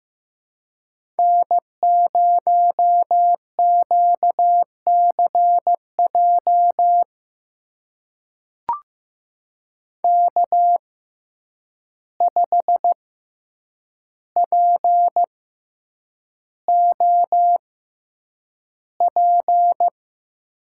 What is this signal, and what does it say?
Morse code, heard as a keyed tone, about 15 wpm, spaced as for 8 wpm, 700 Hz. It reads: N0QCJ K5POP
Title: USA Call Signs - 6x Character Spacing - 15wpm